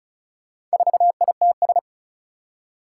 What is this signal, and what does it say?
Morse code, heard as a keyed tone, about 36 wpm, 700 Hz. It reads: 4ITS
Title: Four Challenging Letter-Number Combinations 35wpm